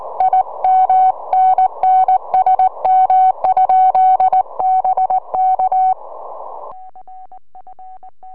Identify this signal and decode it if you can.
Morse code, heard as a keyed tone, about 17 wpm, 750 Hz. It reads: IMNNSM? BK
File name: ea5ef 4m